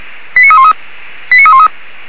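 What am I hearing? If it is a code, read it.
Five-tone selective call (ZVEI), heard as quick runs of five tones, tones of about 80 ms. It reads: 89313 89313
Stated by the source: ZVEI five-tone selective call 89299, received with an SDR.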